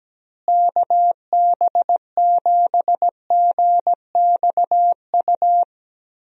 Morse code, heard as a keyed tone, about 17 wpm, 700 Hz. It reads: KB7GXU
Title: USA Call Signs - 6 Characters - Repeated 3x - 17wpm